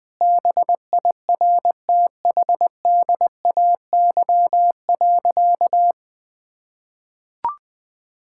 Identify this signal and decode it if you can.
Morse code, heard as a keyed tone, about 20 wpm, 700 Hz. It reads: BIRTHDAY.
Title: Sentences with 6-Word Phrases - Spoken Beginning - 20wpm